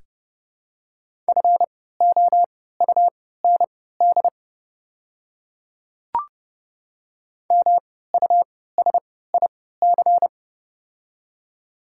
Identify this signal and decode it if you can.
Morse code, heard as a keyed tone, about 31 wpm, spaced as for 17 wpm, 700 Hz. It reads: FOUND MUSIC
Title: 3x Character Spacing - Top 300 Words - 30wpm